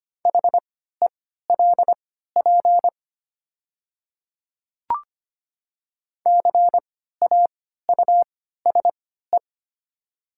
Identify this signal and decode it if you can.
Morse code, heard as a keyed tone, about 25 wpm, spaced as for 14 wpm, 700 Hz. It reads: HELP CAUSE
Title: 3x Character Spacing - Top 200 Words - 25wpm